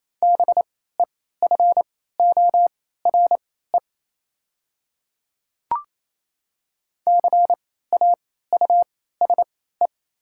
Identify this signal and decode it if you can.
Morse code, heard as a keyed tone, about 28 wpm, spaced as for 16 wpm, 700 Hz. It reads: BEFORE CAUSE
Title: 3x Character Spacing - Top 200 Words - 28wpm